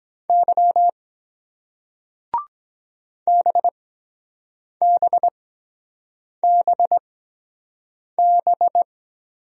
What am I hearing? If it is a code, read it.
Morse code, heard as a keyed tone, about 23 wpm, spaced as for 7 wpm, 700 Hz. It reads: Y BBBB